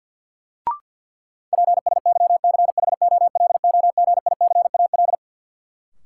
Morse code, heard as a keyed tone, about 52 wpm, 700 Hz. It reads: PSYCHOLOGICAL
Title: Sentences from Top 1400 Words - New Word Drill - 50wpm